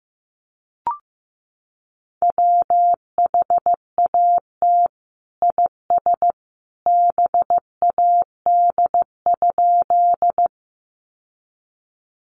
Morse code, heard as a keyed tone, about 14 wpm, 700 Hz. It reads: WHAT IS BAD?
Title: Sentences from Top 700 Words 15wpm